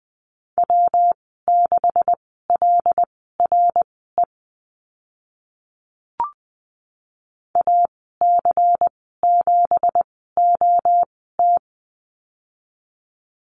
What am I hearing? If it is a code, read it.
Morse code, heard as a keyed tone, about 20 wpm, spaced as for 15 wpm, 700 Hz. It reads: W6LRE AC7OT
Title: 167 Lesson 27g - Number 7 - Call Signs 20wpm